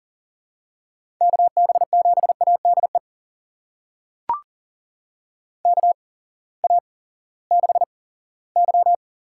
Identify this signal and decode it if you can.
Morse code, heard as a keyed tone, about 39 wpm, spaced as for 20 wpm, 700 Hz. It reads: KB7ADE KA6Y
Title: USA Call Signs - 6 Characters - 8x Character Spacing - 40wpm